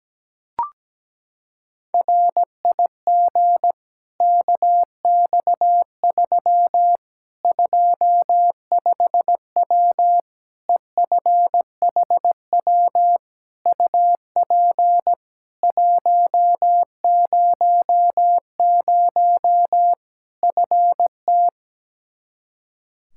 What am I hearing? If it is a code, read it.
Morse code, heard as a keyed tone, about 17 wpm, 700 Hz. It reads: RIG KX3 25W EFHW UP 100 FT
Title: Practice QSO - Exchange 2 - Info - 17wpm